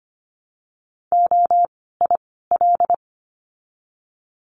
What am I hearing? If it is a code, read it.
Morse code, heard as a keyed tone, about 25 wpm, 700 Hz. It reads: O I L
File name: course / 25:10wpm / 094 Lesson 16f - Letter c - Words 25wpm